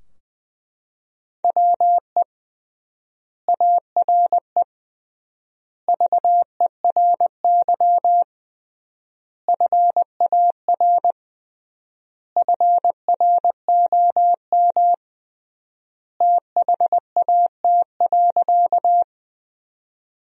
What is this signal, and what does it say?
Morse code, heard as a keyed tone, about 20 wpm, 700 Hz. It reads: WE ARE VERY FAR FROM THAT.
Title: 3x Word-Spacing - Sentences from Top 300 Words - 20wpm